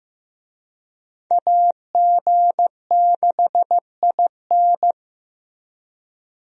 Morse code, heard as a keyed tone, about 15 wpm, 700 Hz. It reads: AG6IN
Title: USA Call Signs - 5 Characters - Repeated 3x - 15wpm